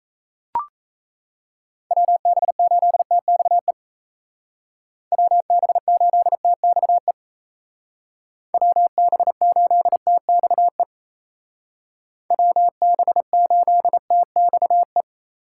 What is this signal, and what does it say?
Morse code, heard as a keyed tone, about 34 wpm, 700 Hz. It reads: WB8TXE WB8TXE WB8TXE WB8TXE
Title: Speed Racing - USA 2x3 Call Signs 28wpm